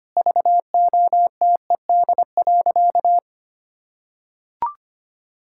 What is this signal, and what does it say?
Morse code, heard as a keyed tone, about 25 wpm, 700 Hz. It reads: VOTED.
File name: practice / 25wpm / Sentences with 3-Word Phrases - 25wpm